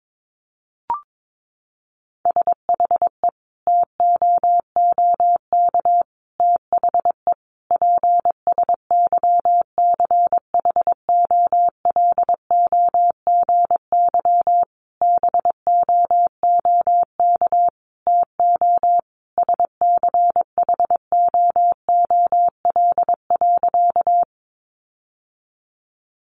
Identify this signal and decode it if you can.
Morse code, heard as a keyed tone, about 22 wpm, 700 Hz. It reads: SHE TOOK THE PSYCHOLOGY BOOK TO SCHOOL.
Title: Sentences from Top 1400 Words 22wpm